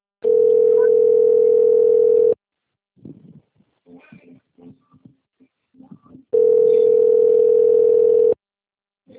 Ringback tone (North America), 440 + 480 Hz.